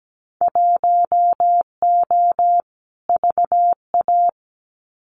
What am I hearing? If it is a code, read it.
Morse code, heard as a keyed tone, about 17 wpm, 700 Hz. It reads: 1O VA
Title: Winter Field Day Exchanges 17wpm